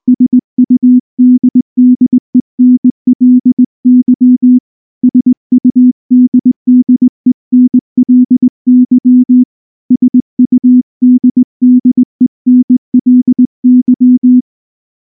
Morse code, heard as a keyed tone, about 19 wpm, 250 Hz. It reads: SUDDENLY SUDDENLY SUDDENLY
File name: morse / morse-training-20WPM-6